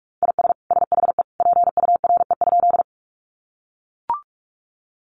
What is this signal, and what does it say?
Morse code, heard as a keyed tone, about 44 wpm, 700 Hz. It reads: IS SHE PURE?
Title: Sentences from Top 1500 Words 45wpm